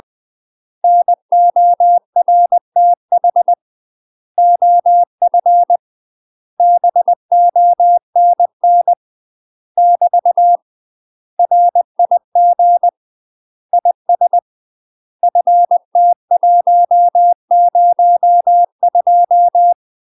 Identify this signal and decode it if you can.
Morse code, heard as a keyed tone, about 20 wpm, 700 Hz. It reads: NORTH OF BONN <BT> RIG IS FT102
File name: UK MASTER QSOs TRAINING / 20WPM / UK MASTER QSOs 0008